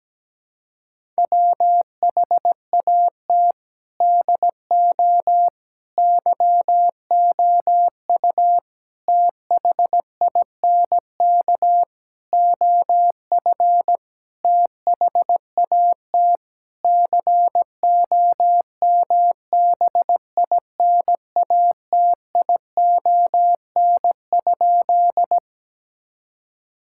Morse code, heard as a keyed tone, about 17 wpm, 700 Hz. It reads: WHAT DO YOU THINK OF THAT COMBINATION?
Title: Sentences from Top 1000 Words - Review Part 6 - 17wpm